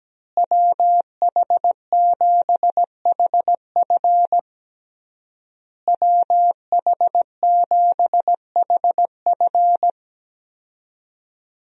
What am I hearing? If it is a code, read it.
Morse code, heard as a keyed tone, about 17 wpm, 700 Hz. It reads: WH7HF WH7HF
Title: USA Call Signs - 5 Characters - Repeated 3x - 17wpm